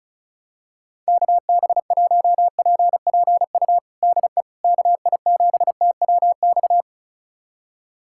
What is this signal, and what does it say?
Morse code, heard as a keyed tone, about 34 wpm, 700 Hz. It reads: KB1PPU DE KI7TWX
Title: USA Callsign DE Callsign - 6 Characters - 35wpm